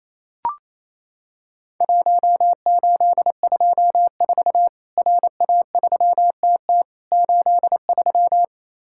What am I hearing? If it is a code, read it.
Morse code, heard as a keyed tone, about 28 wpm, 700 Hz. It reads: 1824 RA3TT 83
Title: Worked All Europe DX Contest - QTC Exchange - 28wpm